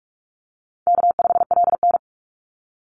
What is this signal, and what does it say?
Morse code, heard as a keyed tone, about 46 wpm, 700 Hz. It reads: K5LN
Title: Easy Call Signs - 45wpm